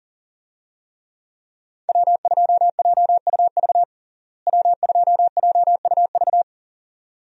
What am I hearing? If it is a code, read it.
Morse code, heard as a keyed tone, about 40 wpm, 700 Hz. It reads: W2JUV W2JUV
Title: USA Call Signs - 5 Characters - Repeated 2x - 40wpm